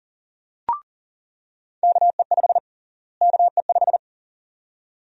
Morse code, heard as a keyed tone, about 40 wpm, 700 Hz. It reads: KE5 KE5
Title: USA Call Sign Prefixes - Repeated 2x - 40wpm